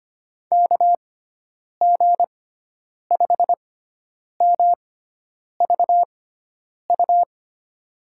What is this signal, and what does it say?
Morse code, heard as a keyed tone, about 25 wpm, spaced as for 9 wpm, 700 Hz. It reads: KG5MVU
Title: USA Call Signs - 6 Characters - 6x Character Spacing - 25wpm